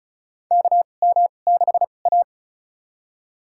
Morse code, heard as a keyed tone, about 36 wpm, spaced as for 25 wpm, 700 Hz. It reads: KM6A
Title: USA Call Signs - 4 Characters - 2x Character Spacing - 35wpm